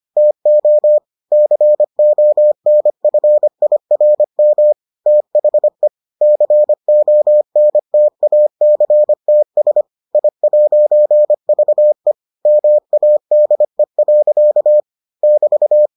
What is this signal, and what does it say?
Morse code, heard as a keyed tone, about 25 wpm, 600 Hz. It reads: TO CONFIRM THE CONTACTS I'VE MADE. <BT>